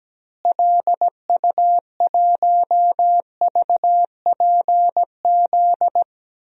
Morse code, heard as a keyed tone, about 17 wpm, 700 Hz. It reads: LU1VPZ
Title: DXCC-100 Call Signs - 6 Characters - Repeated 2x - 17wpm